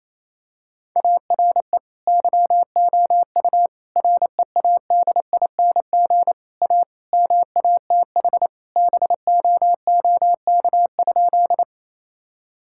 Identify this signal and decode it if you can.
Morse code, heard as a keyed tone, about 28 wpm, 700 Hz. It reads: ARE YOU READING A MATH BOOK?
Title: Sentences from Top 1000 Words - Review Part 6 - 28wpm